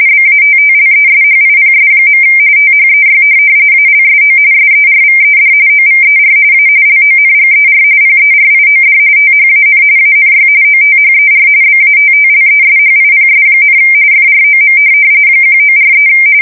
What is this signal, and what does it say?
RTTY, about 45.45 baud. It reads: BATTLESHIPS. THEY ALSO SERVED OVER A LONGER PERIOD THAN ANY OTHER BATTLESHIPS, FROM 1943
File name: fsk170 / USS-New-Jersey